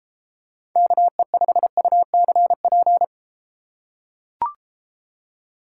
Morse code, heard as a keyed tone, about 33 wpm, 700 Hz. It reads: KE5UCP